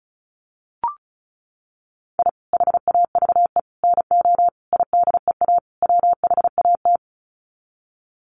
Morse code, heard as a keyed tone, about 34 wpm, 700 Hz. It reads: I HAVE NO IDEA WHAT
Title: Sets of 5 Words Encore 35wpm